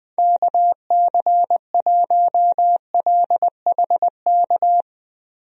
Morse code, heard as a keyed tone, about 20 wpm, 700 Hz. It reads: KC1LHK
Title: USA Call Signs - 6 Characters - Repeated 3x - 20wpm